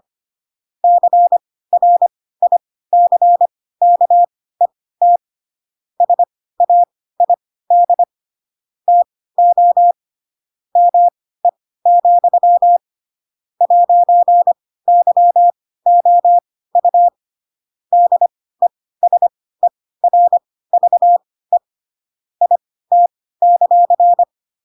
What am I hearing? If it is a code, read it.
Morse code, heard as a keyed tone, about 25 wpm, spaced as for 16 wpm, 700 Hz. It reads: CRICKET SAID TO ME, 'YOU DESERVE IT;
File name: Pinocchio / Pinocchio ch7 10 25 ik1hge